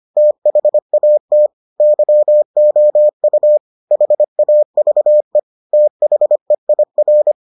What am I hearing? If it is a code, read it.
Morse code, heard as a keyed tone, about 25 wpm, 600 Hz. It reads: THAT YOU HAVE THEIR